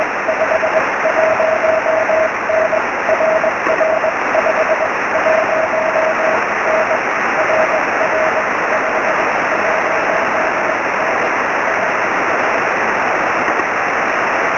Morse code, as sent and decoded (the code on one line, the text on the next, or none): none